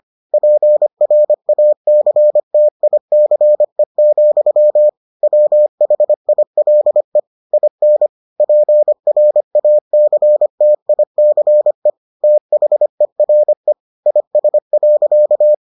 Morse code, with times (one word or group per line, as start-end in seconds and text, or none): 0.3-4.9 PRACTICE,
5.2-7.2 WHILE
7.5-8.1 IN
8.4-11.9 PRACTICE
12.2-13.7 THERE
14.1-15.5 IS.